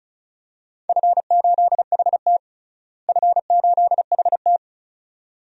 Morse code, heard as a keyed tone, about 34 wpm, 700 Hz. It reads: F8HT F8HT
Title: DXCC-100 Call Signs - 4 Characters - Repeated 2x - 35wpm